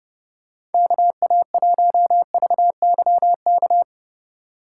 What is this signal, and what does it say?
Morse code, heard as a keyed tone, about 30 wpm, 700 Hz. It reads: KA1VYK
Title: USA Call Signs - 6 Characters - 3x Character Spacing - 30wpm